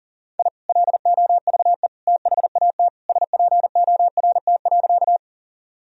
Morse code, heard as a keyed tone, about 40 wpm, 700 Hz. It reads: I LOVE THAT SPORT.
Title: Sentences from Top 1000 Words - Review Part 4 - 40wpm